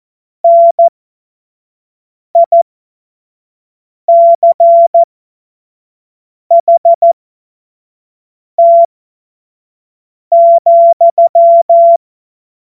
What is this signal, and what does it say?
Morse code, heard as a keyed tone, about 14 wpm, spaced as for 5 wpm, 680 Hz. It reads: NICHT,
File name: Morsekurs / Witze / Witz 25. Januar 2022, 25 F70